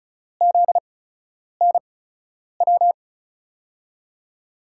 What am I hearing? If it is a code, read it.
Morse code, heard as a keyed tone, about 36 wpm, spaced as for 10 wpm, 700 Hz. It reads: ZNW
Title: USA Call Signs - 6 Characters - 8x Character Spacing - 35wpm